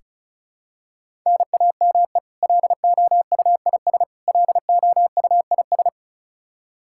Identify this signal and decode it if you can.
Morse code, heard as a keyed tone, about 36 wpm, 700 Hz. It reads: NAME LOUIS LOUIS
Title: K1USN SST QSO Practice 35wpm